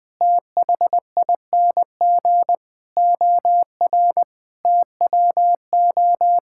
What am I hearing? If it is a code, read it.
Morse code, heard as a keyed tone, about 21 wpm, 700 Hz. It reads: THING OR TWO
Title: Sentences from Top 1000 Words - Review Part 5 - 20wpm